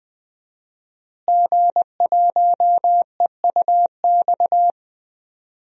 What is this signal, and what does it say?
Morse code, heard as a keyed tone, about 20 wpm, 700 Hz. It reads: G1EUX
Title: DXCC-100 Call Signs - 5 Characters - Repeated 2x - 20wpm